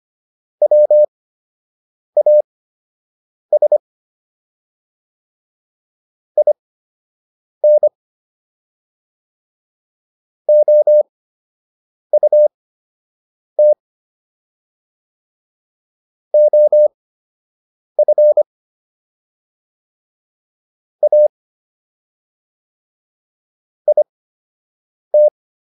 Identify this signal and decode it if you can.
Morse code, heard as a keyed tone, about 25 wpm, spaced as for 7 wpm, 600 Hz. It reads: WAS IN OUT OF A IT